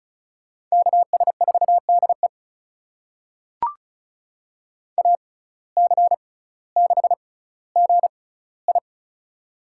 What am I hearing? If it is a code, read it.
Morse code, heard as a keyed tone, about 36 wpm, spaced as for 18 wpm, 700 Hz. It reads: KS4DE AC6GI
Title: USA Call Signs - 5 Characters - 6x Character Spacing - 35wpm